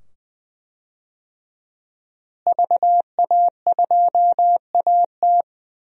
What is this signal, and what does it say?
Morse code, heard as a keyed tone, about 20 wpm, 700 Hz. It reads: VA2AT
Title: DXCC-100 Call Signs - 5 Characters - Repeated 2x - 20wpm